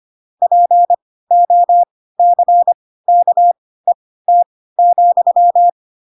Morse code, heard as a keyed tone, about 25 wpm, spaced as for 16 wpm, 700 Hz. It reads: POCKET,
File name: Pinocchio / Pinocchio ch7 10 25 ik1hge